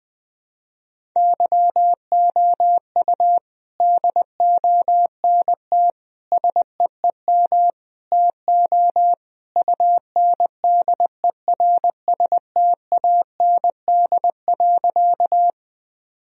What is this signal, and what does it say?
Morse code, heard as a keyed tone, about 21 wpm, 700 Hz. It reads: YOU DONT SEEM TO UNDERSTAND.
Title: Sentences from Top 1000 Words - Review Part 1 - 20wpm